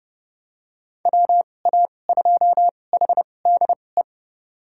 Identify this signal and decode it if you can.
Morse code, heard as a keyed tone, about 30 wpm, spaced as for 22 wpm, 700 Hz. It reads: WA2HDE
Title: USA Call Signs - 6 Characters - 2x Character Spacing - 30wpm